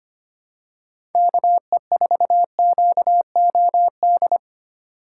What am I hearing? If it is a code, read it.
Morse code, heard as a keyed tone, about 25 wpm, 700 Hz. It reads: KE4QOD
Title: USA Call Signs - 6 Characters - 2x Character Spacing - 25wpm